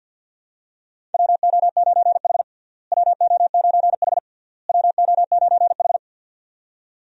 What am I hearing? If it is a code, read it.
Morse code, heard as a keyed tone, about 50 wpm, 700 Hz. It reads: WO9H WO9H WO9H
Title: USA Call Signs - 4 Characters - Repeated 3x - 50wpm